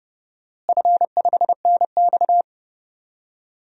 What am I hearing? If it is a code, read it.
Morse code, heard as a keyed tone, about 30 wpm, 700 Hz. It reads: F5NX
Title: DXCC-100 Call Signs - 4 Characters - Repeated 2x - 30wpm